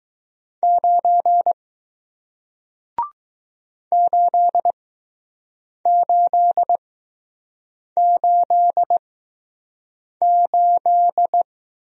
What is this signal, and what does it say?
Morse code, heard as a keyed tone, about 20 wpm, spaced as for 6 wpm, 700 Hz. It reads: 9 8888